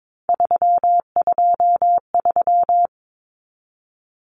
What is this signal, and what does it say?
Morse code, heard as a keyed tone, about 22 wpm, 700 Hz. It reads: 323
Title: Common Cut Numbers - 3-Digit 22wpm